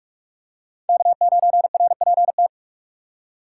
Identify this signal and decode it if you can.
Morse code, heard as a keyed tone, about 46 wpm, 700 Hz. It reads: K9RPT
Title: USA Call Signs - 3x Character Spacing - 45wpm